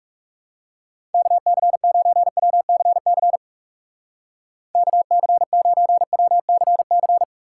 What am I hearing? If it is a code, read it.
Morse code, heard as a keyed tone, about 37 wpm, 700 Hz. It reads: KC9WCC KC9WCC
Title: Speed Racing - USA 2x3 Call Signs 30wpm